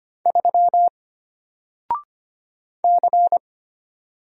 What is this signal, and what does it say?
Morse code, heard as a keyed tone, about 25 wpm, 700 Hz. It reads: 3 C
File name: Single-Letter-Number-25wpm